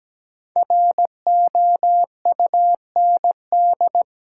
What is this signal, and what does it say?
Morse code, heard as a keyed tone, about 17 wpm, 700 Hz. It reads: ROUND
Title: Sets of 5 Words Encore - New Word Drill - 17wpm